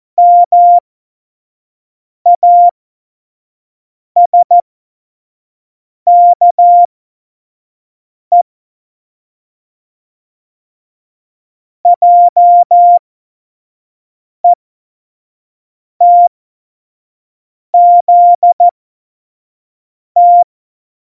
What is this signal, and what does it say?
Morse code, heard as a keyed tone, about 14 wpm, spaced as for 5 wpm, 700 Hz. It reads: MASKE JETZT
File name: Morsekurs / Witze / Witz 23. Januar 2018, 25 F70